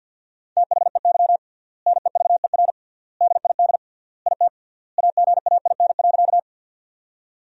Morse code, heard as a keyed tone, about 50 wpm, 700 Hz. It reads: THEY NEVER DID IT AGAIN.
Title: 3x Word-Spacing - Sentences from Top 300 Words - 50wpm